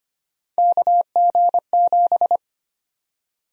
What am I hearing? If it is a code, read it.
Morse code, heard as a keyed tone, about 25 wpm, 700 Hz. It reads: KG7
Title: USA Call Sign Prefixes - 25wpm